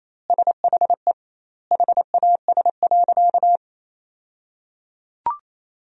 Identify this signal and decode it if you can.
Morse code, heard as a keyed tone, about 28 wpm, 700 Hz. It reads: SHE HAS.
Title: 2x Word-Spacing - Sentences from Top 500 Words - 28wpm